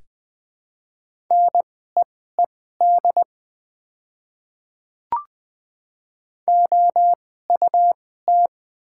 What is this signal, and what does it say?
Morse code, heard as a keyed tone, about 21 wpm, spaced as for 15 wpm, 700 Hz. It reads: NEED OUT
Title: 2x Character Spacing - Top 200 Words - 20wpm